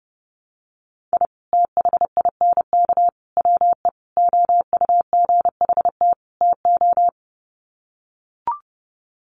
Morse code, heard as a keyed tone, about 30 wpm, 700 Hz. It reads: I THINK WE OUGHT TO
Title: Sets of 5 Words Encore 30wpm